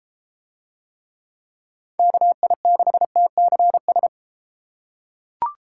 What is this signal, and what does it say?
Morse code, heard as a keyed tone, about 31 wpm, 700 Hz. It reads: KI6TCS